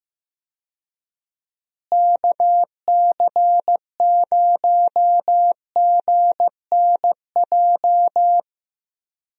Morse code, heard as a keyed tone, about 15 wpm, 700 Hz. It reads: KC0GNJ
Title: USA Call Signs - 6 Characters - Repeated 3x - 15wpm